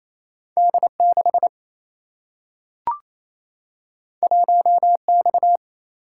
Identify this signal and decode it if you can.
Morse code, heard as a keyed tone, about 28 wpm, 700 Hz. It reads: D6 1X
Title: Single-Letter Single-Number 28wpm